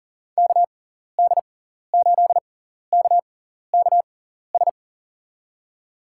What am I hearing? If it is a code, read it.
Morse code, heard as a keyed tone, about 40 wpm, spaced as for 14 wpm, 700 Hz. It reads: KD8KKS
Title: USA Call Signs - 6 Characters - 6x Character Spacing - 40wpm